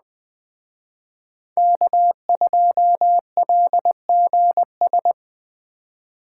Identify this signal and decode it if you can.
Morse code, heard as a keyed tone, about 20 wpm, 700 Hz. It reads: K2LGS
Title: Speed Racing - USA 1x3 Call Signs 17wpm